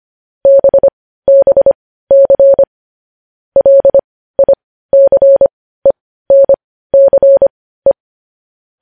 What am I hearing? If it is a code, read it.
Morse code, heard as a keyed tone, about 25 wpm, spaced as for 15 wpm, 550 Hz. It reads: BBC LICENCE
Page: news-headlines-in-morse-code-at-15-wpm